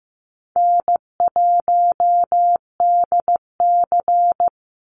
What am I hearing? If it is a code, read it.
Morse code, heard as a keyed tone, about 15 wpm, 700 Hz. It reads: N1DC
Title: Easy Call Signs - 15wpm